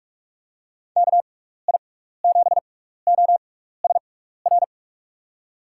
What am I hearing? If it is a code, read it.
Morse code, heard as a keyed tone, about 41 wpm, spaced as for 15 wpm, 700 Hz. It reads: KI7OSR
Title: USA Call Signs - 6 Characters - 6x Character Spacing - 45wpm